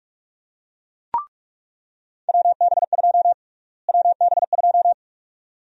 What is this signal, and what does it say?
Morse code, heard as a keyed tone, about 46 wpm, 700 Hz. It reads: WB2 WB2
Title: USA Call Sign Prefixes - Repeated 2x - 45wpm